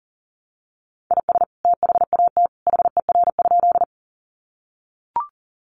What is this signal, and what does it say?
Morse code, heard as a keyed tone, about 40 wpm, 700 Hz. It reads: IS THAT HER?
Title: Sentences from Top 100 Words 40wpm